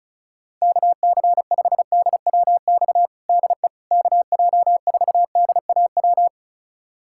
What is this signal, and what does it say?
Morse code, heard as a keyed tone, about 36 wpm, 700 Hz. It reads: KC5DWX DE KJ4DAW
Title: USA Callsign DE Callsign - 6 Characters - 35wpm